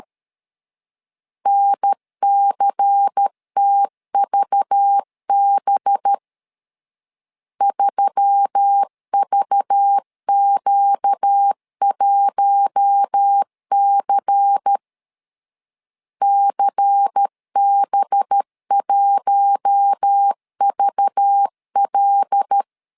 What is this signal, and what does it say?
Morse code, heard as a keyed tone, about 13 wpm, 800 Hz. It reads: NCTVB 3VQ1C CB1VL
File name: LEZIONI / 5 LBVC / 5 LBVC 12wpm 1955